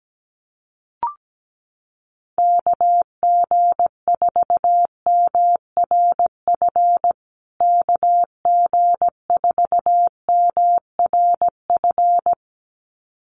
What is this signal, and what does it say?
Morse code, heard as a keyed tone, about 17 wpm, 700 Hz. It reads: KG4MRF KG4MRF
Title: USA Call Signs Repeated 2x 17wpm